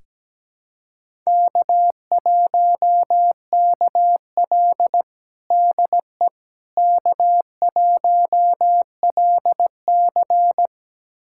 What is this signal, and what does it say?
Morse code, heard as a keyed tone, about 17 wpm, 700 Hz. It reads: K1KL DE K1LC
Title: USA Callsign DE Callsign - 4 Characters - 17wpm